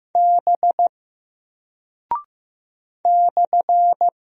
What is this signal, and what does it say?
Morse code, heard as a keyed tone, about 15 wpm, 700 Hz. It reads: B /
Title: Single Character (Letter:Number:Punctuation:Prosign) Alt 15wpm